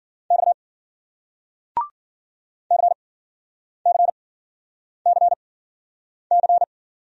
Morse code, heard as a keyed tone, about 56 wpm, spaced as for 9 wpm, 700 Hz. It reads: X CCCC